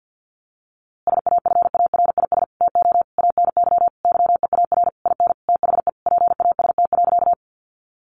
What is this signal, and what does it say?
Morse code, heard as a keyed tone, about 50 wpm, 700 Hz. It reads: SAFARIS TO ANY YEAR IN THE PAST.
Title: A Sound Of Thunder 50wpm